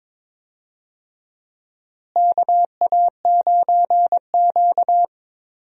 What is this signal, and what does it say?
Morse code, heard as a keyed tone, about 21 wpm, 700 Hz. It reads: KA9Q
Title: USA Call Signs - 4 Characters - Repeated 2x - 22wpm